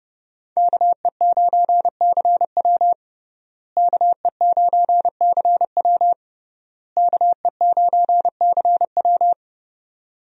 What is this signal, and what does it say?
Morse code, heard as a keyed tone, about 30 wpm, 700 Hz. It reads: KE9CW KE9CW KE9CW
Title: USA Call Signs - 5 Characters - Repeated 3x - 30wpm